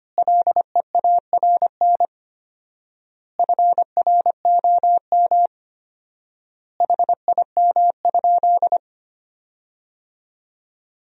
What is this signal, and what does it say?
Morse code, heard as a keyed tone, about 25 wpm, 700 Hz. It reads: LEARN FROM HIM?
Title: 4x Word-Spacing - Sentences from Top 300 Words - 25wpm